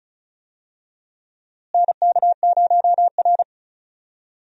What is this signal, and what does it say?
Morse code, heard as a keyed tone, about 36 wpm, 700 Hz. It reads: NK0R
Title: USA Call Signs - 4 Characters - Repeated 2x - 35wpm